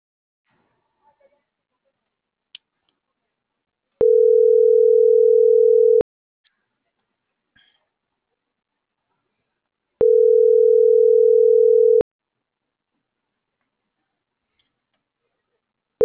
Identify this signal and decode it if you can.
Ringback tone (North America), 440 + 480 Hz.